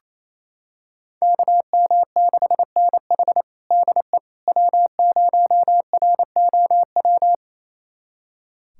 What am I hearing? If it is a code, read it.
Morse code, heard as a keyed tone, about 28 wpm, 700 Hz. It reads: KM6NH DE W0ROW
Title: USA Callsign DE Callsign - 5 Characters - 28wpm